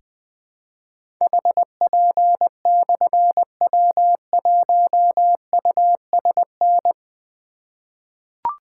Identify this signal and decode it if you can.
Morse code, heard as a keyed tone, about 20 wpm, 700 Hz. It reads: HP/W1USN
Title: Difficult Call Signs 20wpm